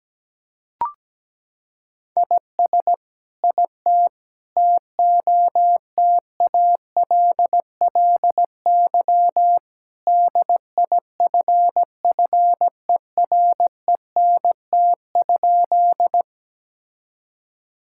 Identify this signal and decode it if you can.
Morse code, heard as a keyed tone, about 17 wpm, 700 Hz. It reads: IS IT TOTALLY DIFFERENT?
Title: Sentences from Top 1600 Words 17wpm